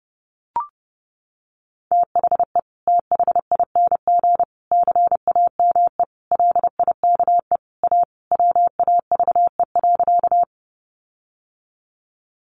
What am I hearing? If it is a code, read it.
Morse code, heard as a keyed tone, about 30 wpm, 700 Hz. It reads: THE THING CAME LIKE A WAVE.
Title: Sentences from Top 1400 Words 30wpm